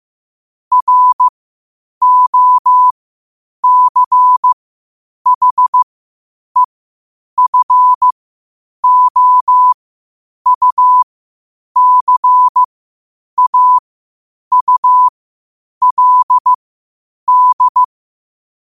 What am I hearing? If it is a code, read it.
Morse code, heard as a keyed tone, about 15 wpm, spaced as for 8 wpm, 1000 Hz. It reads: ROCHEFOUCAULD
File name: qotdfiles / 8wpm